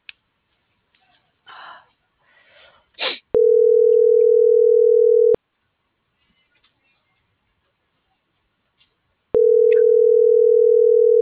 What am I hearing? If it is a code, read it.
Ringback tone (North America), 440 + 480 Hz.